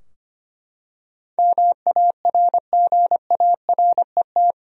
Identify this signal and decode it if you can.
Morse code, heard as a keyed tone, about 25 wpm, 700 Hz. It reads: MARGARET
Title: USA Common Names - 25wpm